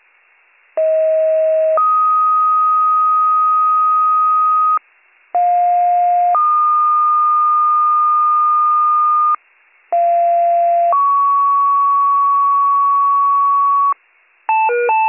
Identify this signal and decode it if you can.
Two-tone page, heard as a one-second tone, then a three-second tone, about 640 then 1250 Hz, about 710 then 1220 Hz, about 690 then 1120 Hz. It is heard from 0.8 to 13.9 s.